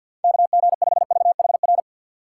Morse code, heard as a keyed tone, about 53 wpm, 700 Hz. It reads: KG5VHR
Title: USA Call Signs - 4x Character Spacing - 50wpm